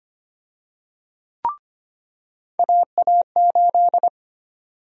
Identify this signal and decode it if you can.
Morse code, heard as a keyed tone, about 25 wpm, 700 Hz. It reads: AA8
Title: USA Call Sign Prefixes - 25wpm